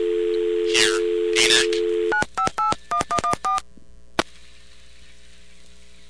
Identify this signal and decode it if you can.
DTMF, heard as telephone touch tones, tones of about 110 ms. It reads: 9581114